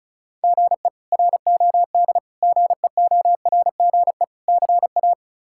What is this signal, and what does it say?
Morse code, heard as a keyed tone, about 34 wpm, 700 Hz. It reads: GE ROD GEORGE CA
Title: K1USN SST QSO Practice - 35wpm